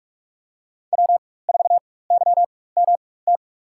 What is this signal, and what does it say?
Morse code, heard as a keyed tone, about 46 wpm, spaced as for 21 wpm, 700 Hz. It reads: W4YMT
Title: USA Call Signs - 5 Characters - 4x Character Spacing - 45wpm